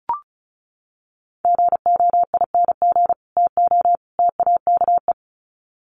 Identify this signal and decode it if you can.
Morse code, heard as a keyed tone, about 34 wpm, 700 Hz. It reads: GOING TO TAKE
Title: Sets of 3 Words 35wpm